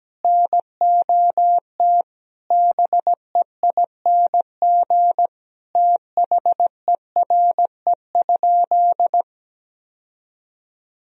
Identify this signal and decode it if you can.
Morse code, heard as a keyed tone, about 17 wpm, 700 Hz. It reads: NOT BEING THERE?
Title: Sentences from Top 1000 Words - Review Part 3 - 17wpm